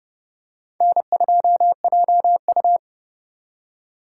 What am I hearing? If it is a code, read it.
Morse code, heard as a keyed tone, about 29 wpm, 700 Hz. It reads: N2JU